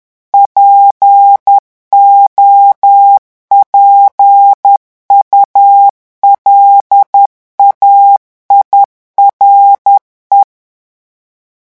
Morse code, heard as a keyed tone, about 10 wpm, 800 Hz. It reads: POPULAIRE
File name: morse mp3 / revision francais 10 mots minute